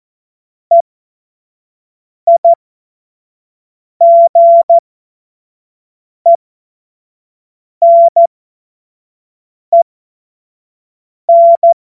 Morse code, heard as a keyed tone, about 14 wpm, spaced as for 5 wpm, 680 Hz. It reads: EIGENEN
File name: Morsekurs / Witze / Witz 03. Mai 2022 25 F70